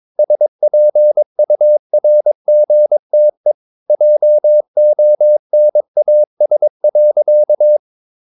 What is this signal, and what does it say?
Morse code, heard as a keyed tone, about 22 wpm, 600 Hz. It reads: SPURGTE JONAS.
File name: Ferie QRS 22wpm